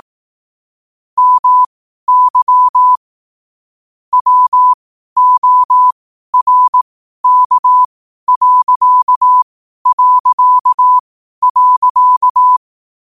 Morse code, heard as a keyed tone, about 18 wpm, spaced as for 12 wpm, 1000 Hz. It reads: MY WORK...
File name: qotdfiles / 12wpm